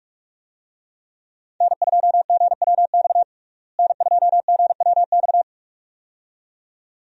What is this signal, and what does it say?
Morse code, heard as a keyed tone, about 46 wpm, 700 Hz. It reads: N2GWX N2GWX
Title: USA Call Signs - 5 Characters - Repeated 2x - 45wpm